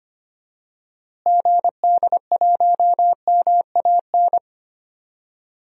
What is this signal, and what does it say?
Morse code, heard as a keyed tone, about 25 wpm, 700 Hz. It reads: GD1MAN
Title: DXCC-100 Call Signs - 6 Characters - Repeated 3x - 25wpm